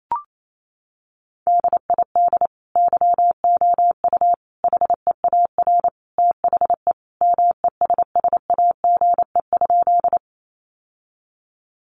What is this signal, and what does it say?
Morse code, heard as a keyed tone, about 28 wpm, 700 Hz. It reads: DID YOU HEAR THE MESSAGE?
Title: Sentences from Top 1100 Words 28wpm